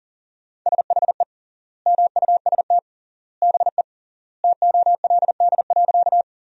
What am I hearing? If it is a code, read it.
Morse code, heard as a keyed tone, about 40 wpm, 700 Hz. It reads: SHE MUST BE TOLD.
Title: 3x Word-Spacing - Sentences from Top 400 Words - 40wpm